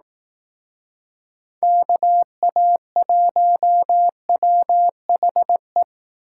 Morse code, heard as a keyed tone, about 19 wpm, 700 Hz. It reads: KA1WHE
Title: Speed Racing - USA 2x3 Call Signs 15wpm